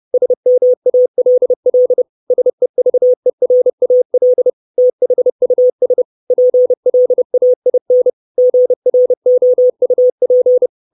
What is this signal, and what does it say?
Morse code, heard as a keyed tone, about 30 wpm, 500 Hz. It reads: SMALL SEVERAL THUS PLAIN GROUP
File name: 632 Words 30WPM